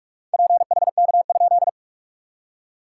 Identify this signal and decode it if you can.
Morse code, heard as a keyed tone, about 46 wpm, 660 Hz. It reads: PHK?
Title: SOTA QSO Practice 45wpm